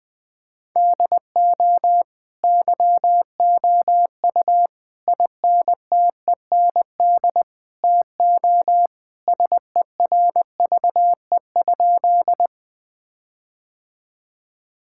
Morse code, heard as a keyed tone, about 21 wpm, 700 Hz. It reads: DO YOU INTEND TO SERVE?
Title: Sentences from Top 1600 Words 20wpm